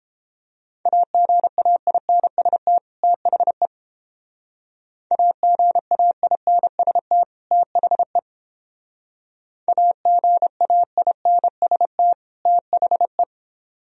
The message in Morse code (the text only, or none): AGAINST THE AGAINST THE AGAINST THE